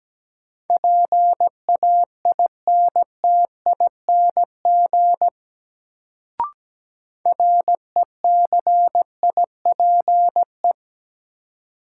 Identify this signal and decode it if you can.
Morse code, heard as a keyed tone, about 17 wpm, 700 Hz. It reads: PAINTING RECIPE
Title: Sentences from Top 1200 Words - New Word Drill Alt - 17wpm